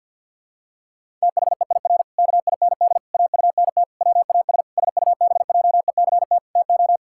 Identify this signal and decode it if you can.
Morse code, heard as a keyed tone, about 50 wpm, 700 Hz. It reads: THEIR KIND AUNT WAS SUBJECT TO
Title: Pangram Challenge v2 - 50wpm